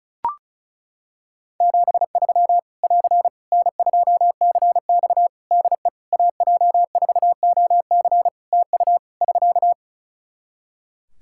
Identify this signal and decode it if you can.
Morse code, heard as a keyed tone, about 36 wpm, 700 Hz. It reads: 73 <AR> N2CX DE AJ4OC TU <SK>
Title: Practice QSO - Exchange 4 - Ending - 35wpm